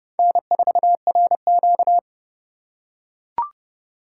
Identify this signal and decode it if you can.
Morse code, heard as a keyed tone, about 28 wpm, 700 Hz. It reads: N4RQ